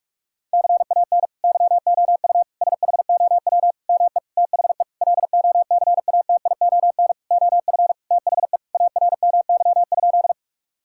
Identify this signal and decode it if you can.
Morse code, heard as a keyed tone, about 46 wpm, 700 Hz. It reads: CAN YOU SHOW ME THE LOCATION OF THE ARMY?
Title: Sentences from Top 1000 Words - Review Part 6 - 45wpm